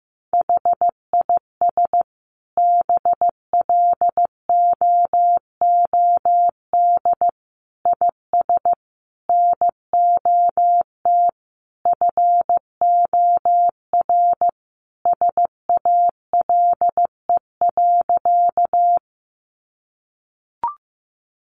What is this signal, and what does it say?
Morse code, heard as a keyed tone, about 15 wpm, 700 Hz. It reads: HIS BLOOD IS NOT FOR SALE.
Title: Sentences from Top 1400 Words 15wpm